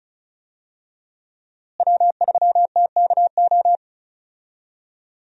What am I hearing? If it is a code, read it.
Morse code, heard as a keyed tone, about 36 wpm, 700 Hz. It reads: W3TKO
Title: USA Call Signs - 5 Characters - Repeated 3x - 35wpm